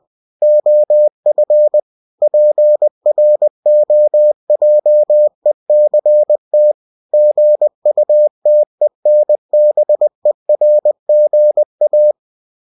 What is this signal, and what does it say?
Morse code, heard as a keyed tone, about 20 wpm, 600 Hz. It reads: OF PROJECT GUTENBERGA
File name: cw-books / winnie-the-pooh / winnie-20wpm-page-0117